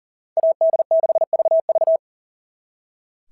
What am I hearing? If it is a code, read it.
Morse code, heard as a keyed tone, about 38 wpm, 660 Hz. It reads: AD6VV
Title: SOTA QSO Practice 40wpm